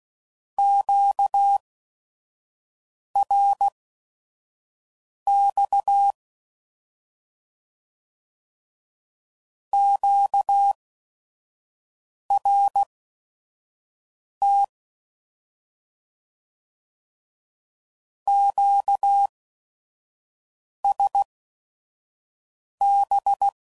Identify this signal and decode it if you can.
Morse code, heard as a keyed tone, about 16 wpm, spaced as for 5 wpm, 800 Hz. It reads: QRX QRT QSB
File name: Mixed / Misc / Words